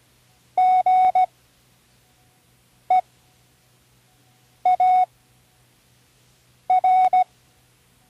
Morse code, heard as a keyed tone, about 16 wpm, spaced as for 5 wpm, 740 Hz. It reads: GEAR
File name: Learning Morse Code (from Tune in the World)